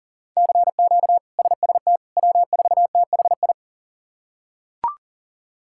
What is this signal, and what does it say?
Morse code, heard as a keyed tone, about 40 wpm, 700 Hz. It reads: CQ SST W4THI
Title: K1USN SST QSO Practice - 40wpm